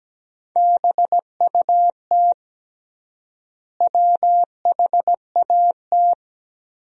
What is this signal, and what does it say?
Morse code, heard as a keyed tone, about 17 wpm, 700 Hz. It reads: BUT WHAT
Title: 3x Word-Spacing - Sentences from Top 100 Words - 17wpm